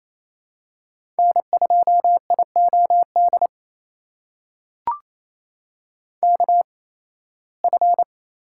Morse code, heard as a keyed tone, about 28 wpm, spaced as for 18 wpm, 700 Hz. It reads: N2IOD KF
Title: USA Call Signs - 5 Characters - 8x Character Spacing - 28wpm